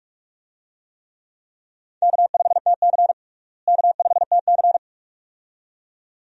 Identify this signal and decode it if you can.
Morse code, heard as a keyed tone, about 42 wpm, 700 Hz. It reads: K5TC K5TC
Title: USA Call Signs - 4 Characters - Repeated 2x - 45wpm